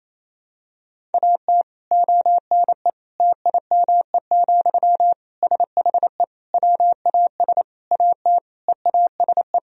Morse code, heard as a keyed tone, about 28 wpm, 700 Hz. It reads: AT ONE TIME, SHE WAS AT EASE
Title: Sentences from Top 1000 Words - Review Part 1 - 28wpm